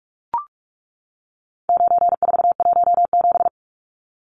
Morse code, heard as a keyed tone, about 44 wpm, 700 Hz. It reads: 9417
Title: Numbers - 4-Digit 45wpm